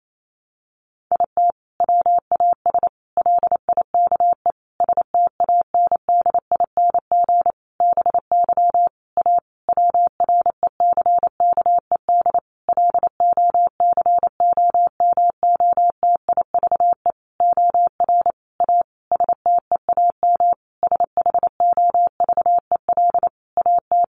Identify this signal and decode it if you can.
Morse code, heard as a keyed tone, about 28 wpm, 700 Hz. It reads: IT WAS LIKE STANDING BY A WRECKED LOCOMOTIVE OR A STEAM SHOVEL AT
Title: A Sound Of Thunder 28wpm